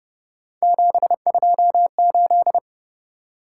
Morse code, heard as a keyed tone, about 31 wpm, 700 Hz. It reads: 728
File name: Speed-Racing - Three Letter-Number Combinations - 25wpm